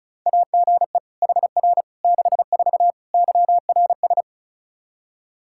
Morse code, heard as a keyed tone, about 34 wpm, 700 Hz. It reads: AGE HR 64 YRS
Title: K1USN SST QSO Practice 35wpm